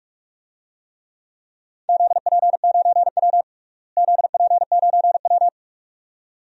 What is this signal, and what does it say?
Morse code, heard as a keyed tone, about 46 wpm, 700 Hz. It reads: ZP9W ZP9W
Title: DXCC-100 Call Signs - 4 Characters - Repeated 2x - 45wpm